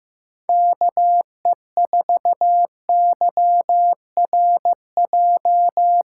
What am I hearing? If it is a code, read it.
Morse code, heard as a keyed tone, about 15 wpm, 700 Hz. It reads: KE4YRJ
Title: USA Callsign DE Callsign - 6 Characters - 15wpm